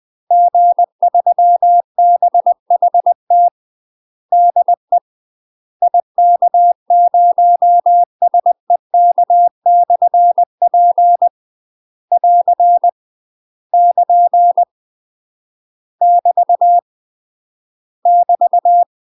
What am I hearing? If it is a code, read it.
Morse code, heard as a keyed tone, about 20 wpm, 700 Hz. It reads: G3BHT DE IK0SEK/P <AR> <KN> <BT> <BT>
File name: UK MASTER QSOs TRAINING / 20WPM / UK MASTER QSOs 0031